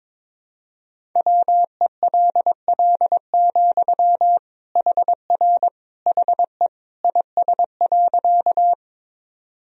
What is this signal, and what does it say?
Morse code, heard as a keyed tone, about 22 wpm, 700 Hz. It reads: WELL, HR HE IS.
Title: Sentences from Top 200 Words and Abbreviations 22wpm